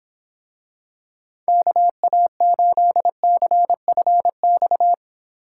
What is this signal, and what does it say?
Morse code, heard as a keyed tone, about 26 wpm, 700 Hz. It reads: KA8CFX